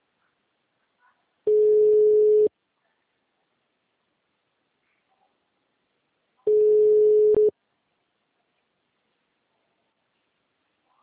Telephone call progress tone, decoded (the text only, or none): ringback tone